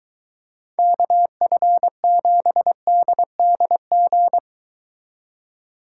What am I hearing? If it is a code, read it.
Morse code, heard as a keyed tone, about 23 wpm, 700 Hz. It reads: KF7DDG